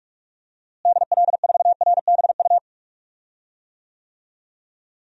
Morse code, heard as a keyed tone, about 47 wpm, 700 Hz. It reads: DL4RBU
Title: DXCC-100 Call Signs - 6 Characters - Repeated 2x - 45wpm